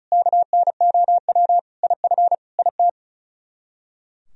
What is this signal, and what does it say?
Morse code, heard as a keyed tone, about 44 wpm, 700 Hz. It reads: KNOW IF IT
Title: Speed-Racing - Sets of 5 Words 35wpm